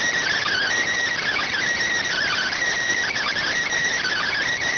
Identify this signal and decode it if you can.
Emergency Alert System (SAME) tones, heard as harsh buzzing bursts. It is heard from 0.3 to 4.4 s.